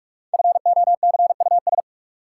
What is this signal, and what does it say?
Morse code, heard as a keyed tone, about 46 wpm, 700 Hz. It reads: FOCUS
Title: Sentences from Top 1000 Words - Review Part 4 - New Word Drill - 45wpm